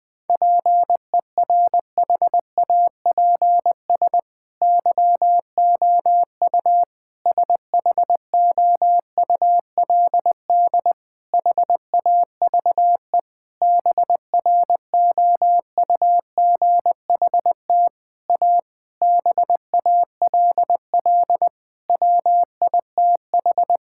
Morse code, heard as a keyed tone, about 20 wpm, 700 Hz. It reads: PERHAPS YOU SHOULD HAVE BROUGHT A BALL WITH
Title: Sentences from Top 1000 Words - Review Part 6 - 20wpm